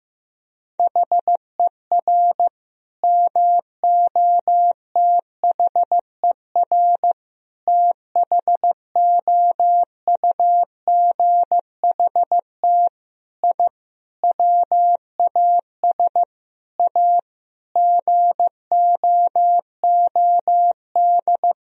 Morse code, heard as a keyed tone, about 16 wpm, 700 Hz. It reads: HER MOTHER THOUGHT I WAS A GOOD
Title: Sentences from Top 1000 Words - Review Part 4 - 15wpm